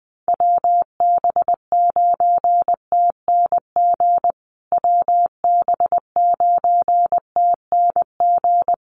Morse code, heard as a keyed tone, about 20 wpm, 700 Hz. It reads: WB9TNG WB9TNG
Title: USA Call Signs Repeated 3x 20wpm